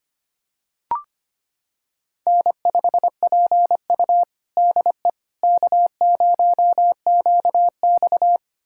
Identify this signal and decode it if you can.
Morse code, heard as a keyed tone, about 25 wpm, 700 Hz. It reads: N5PU DE K0QX
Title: USA Callsign DE Callsign - 4 Characters - 25wpm